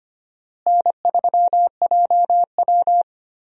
Morse code, heard as a keyed tone, about 25 wpm, 700 Hz. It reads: N3JW
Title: USA Call Signs - 4 Characters - Repeated 3x - 25wpm